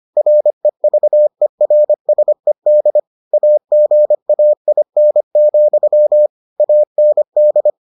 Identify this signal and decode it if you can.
Morse code, heard as a keyed tone, about 25 wpm, 600 Hz. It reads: REVERSED AGAIN, AND